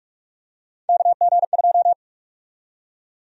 Morse code, heard as a keyed tone, about 46 wpm, 690 Hz. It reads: KG2
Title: USA Call Sign Prefixes - Repeated 2x - 45wpm